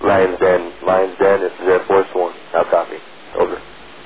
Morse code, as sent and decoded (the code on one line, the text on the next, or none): none